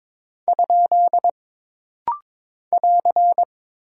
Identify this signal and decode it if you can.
Morse code, heard as a keyed tone, about 22 wpm, 700 Hz. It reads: ? <AR>
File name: practice / 22wpm / Single Punctuation-Prosign - Rapid-Fire - 22wpm